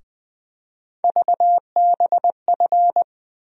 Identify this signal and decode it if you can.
Morse code, heard as a keyed tone, about 20 wpm, 700 Hz. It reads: VBF
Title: USA Call Sign Postfixes - 3 Letters 20wpm